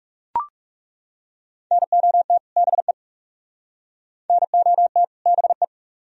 Morse code, heard as a keyed tone, about 37 wpm, 700 Hz. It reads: NOT BE NOT BE
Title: Speed-Racing - Sets of 2 Words 30wpm